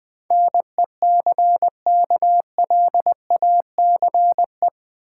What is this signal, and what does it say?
Morse code, heard as a keyed tone, about 20 wpm, 700 Hz. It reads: NECKLACE
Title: Easy Words with Hints - 20wpm